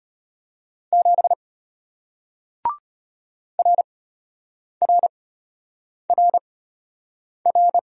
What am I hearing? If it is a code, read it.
Morse code, heard as a keyed tone, about 33 wpm, spaced as for 8 wpm, 690 Hz. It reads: 7 RRRR